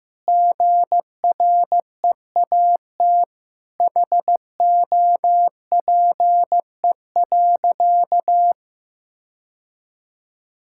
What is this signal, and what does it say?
Morse code, heard as a keyed tone, about 15 wpm, 700 Hz. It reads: GREAT HOPE.
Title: Sentences from Top 1000 Words - Review Part 5 - 15wpm